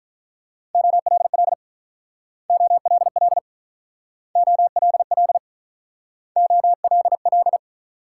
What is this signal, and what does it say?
Morse code, heard as a keyed tone, about 45 wpm, 700 Hz. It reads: OLL OLL OLL OLL